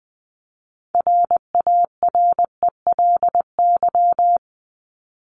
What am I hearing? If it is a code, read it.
Morse code, heard as a keyed tone, about 20 wpm, 700 Hz. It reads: RARELY
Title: Top 1251-1500 Words 20wpm